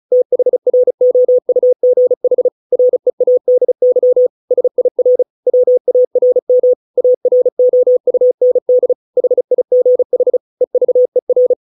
Morse code, heard as a keyed tone, about 34 wpm, 500 Hz. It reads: THROUGH READY SIR WARM AROUND HIGH EVER